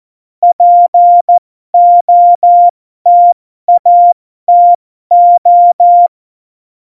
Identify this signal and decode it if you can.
Morse code, heard as a keyed tone, about 14 wpm, spaced as for 12 wpm, 700 Hz. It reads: POTATO
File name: Morsekurs / Witze